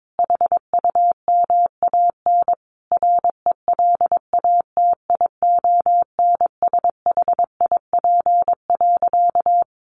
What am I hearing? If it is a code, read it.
Morse code, heard as a keyed tone, about 22 wpm, 700 Hz. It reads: HUMAN RELATIONSHIP.
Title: Sentences from Top 1100 Words 22wpm